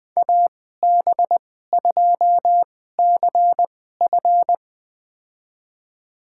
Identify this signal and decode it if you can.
Morse code, heard as a keyed tone, about 20 wpm, spaced as for 14 wpm, 700 Hz. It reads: AB2CF
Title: USA Call Signs - 5 Characters - 2x Character Spacing - 20wpm